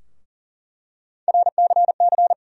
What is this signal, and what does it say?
Morse code, heard as a keyed tone, about 40 wpm, 700 Hz. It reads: RCC
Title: USA Call Sign Postfixes - 3 Letters 40wpm